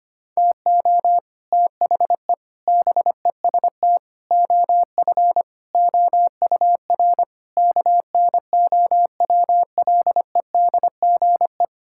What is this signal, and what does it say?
Morse code, heard as a keyed tone, about 25 wpm, 700 Hz. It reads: TO THE BEST OF OUR KNOWLEDGE
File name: Sets of 9 Words - 25wpm